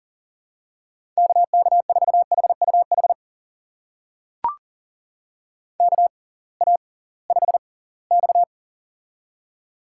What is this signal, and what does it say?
Morse code, heard as a keyed tone, about 40 wpm, spaced as for 23 wpm, 700 Hz. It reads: KK4HUH KA5X
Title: USA Call Signs - 6x Character Spacing - 40wpm